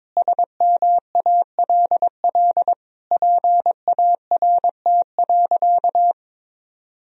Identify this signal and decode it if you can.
Morse code, heard as a keyed tone, about 22 wpm, 700 Hz. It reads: SMALL PART.
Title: Sentences from Top 1000 Words - Review Part 1 - 22wpm